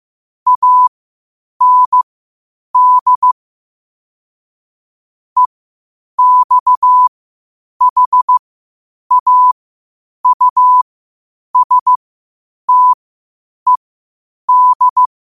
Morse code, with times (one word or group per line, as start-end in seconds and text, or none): 0.5-3.3 AND
5.4-15.1 EXHAUSTED